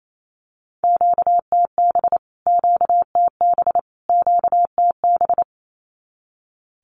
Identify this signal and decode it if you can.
Morse code, heard as a keyed tone, about 28 wpm, 700 Hz. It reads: QTB QTB QTB
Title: Q-Codes Repeated 3x 28wpm